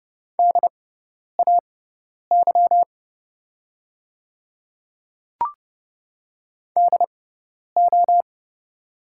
Morse code, heard as a keyed tone, about 30 wpm, spaced as for 10 wpm, 700 Hz. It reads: DAY DO
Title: 6x Character Spacing - Top 100 Words - 30wpm